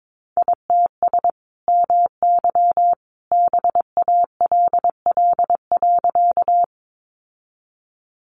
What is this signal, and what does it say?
Morse code, heard as a keyed tone, about 22 wpm, 700 Hz. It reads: ITS MY BALL.
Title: Sentences from Top 900 Words 22wpm